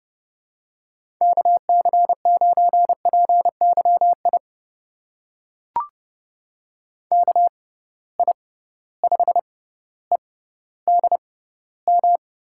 Morse code, heard as a keyed tone, about 30 wpm, spaced as for 15 wpm, 700 Hz. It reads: KC9PYI KI5EDM
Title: USA Call Signs - 6 Characters - 6x Character Spacing - 30wpm